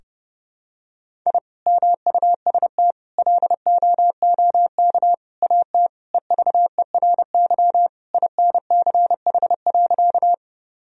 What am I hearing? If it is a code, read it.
Morse code, heard as a keyed tone, about 29 wpm, 700 Hz. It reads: I MUST LOOK AT EVERY INCH.
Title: Sentences from Top 1000 Words - Review Part 3 - 30wpm